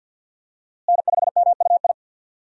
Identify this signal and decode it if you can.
Morse code, heard as a keyed tone, about 51 wpm, 700 Hz. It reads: N5MUI
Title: USA Call Signs - 5 Characters - Repeated 2x - 50wpm